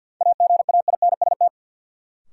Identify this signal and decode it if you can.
Morse code, heard as a keyed tone, about 53 wpm, 700 Hz. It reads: AGAINST
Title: Sentences from Top 400 Words - New Word Drill - 50wpm